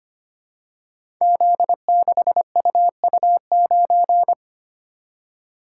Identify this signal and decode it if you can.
Morse code, heard as a keyed tone, about 25 wpm, 700 Hz. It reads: Z6UU9
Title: Code Groups - 5 Letter:Numbers - 25wpm